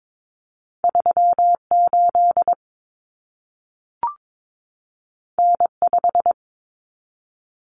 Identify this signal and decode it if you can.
Morse code, heard as a keyed tone, about 22 wpm, 700 Hz. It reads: 38 N5
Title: Common Cut Numbers - 2-Digit 22wpm